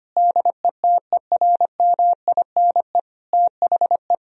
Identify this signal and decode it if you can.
Morse code, heard as a keyed tone, about 25 wpm, 700 Hz. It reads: DETERMINE THE
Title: Sentences from Top 1000 Words - Review Part 4 - 25wpm